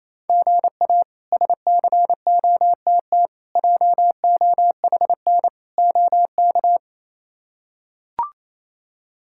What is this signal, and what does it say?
Morse code, heard as a keyed tone, about 28 wpm, 700 Hz. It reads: GA SCOTT JOHN OK
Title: K1USN SST QSO Practice - 28wpm